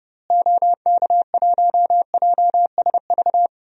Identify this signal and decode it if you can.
Morse code, heard as a keyed tone, about 30 wpm, 700 Hz. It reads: OK1JSV
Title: DXCC-100 Call Signs - 6 Characters - Repeated 2x - 30wpm